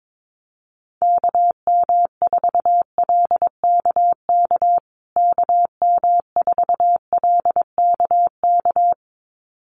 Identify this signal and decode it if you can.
Morse code, heard as a keyed tone, about 22 wpm, 700 Hz. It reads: KM4LKK KM4LKK
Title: USA Call Signs Repeated 2x 22wpm